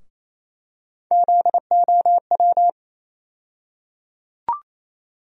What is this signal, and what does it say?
Morse code, heard as a keyed tone, about 28 wpm, 700 Hz. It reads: ZOW
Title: USA Call Sign Postfixes - 3 Letters 28wpm